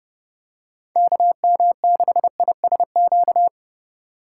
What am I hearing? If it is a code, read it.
Morse code, heard as a keyed tone, about 29 wpm, 700 Hz. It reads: KM6ISQ